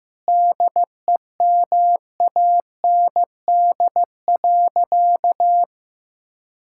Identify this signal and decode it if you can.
Morse code, heard as a keyed tone, about 15 wpm, 700 Hz. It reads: DEMAND.
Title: Sentences from Top 1000 Words - Review Part 5 - 15wpm